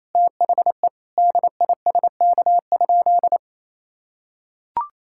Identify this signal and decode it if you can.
Morse code, heard as a keyed tone, about 27 wpm, 700 Hz. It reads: THE DISK?
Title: Sentences from Top 1000 Words - Review Part 6 - 28wpm